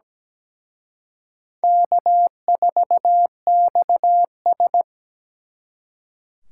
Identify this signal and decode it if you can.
Morse code, heard as a keyed tone, about 20 wpm, 700 Hz. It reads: K4XS
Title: Speed Racing - USA 1x2 Call Signs 17wpm